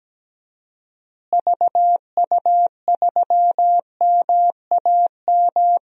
Morse code, heard as a keyed tone, about 17 wpm, 700 Hz. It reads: VU3MAM
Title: DXCC-100 Call Signs - 6 Characters - Repeated 2x - 17wpm